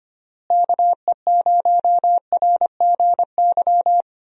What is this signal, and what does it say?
Morse code, heard as a keyed tone, about 25 wpm, 700 Hz. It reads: KE0RGY
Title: USA Call Signs - 6 Characters - Repeated 2x - 25wpm